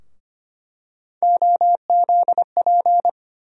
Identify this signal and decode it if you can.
Morse code, heard as a keyed tone, about 25 wpm, 700 Hz. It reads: OZP
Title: USA Call Sign Postfixes - 3 Letters 25wpm